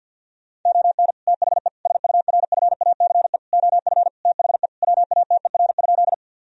Morse code, heard as a keyed tone, about 50 wpm, 700 Hz. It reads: ON THE SURFACE OF THE WATER?
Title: Sentences from Top 1000 Words - Review Part 3 - 50wpm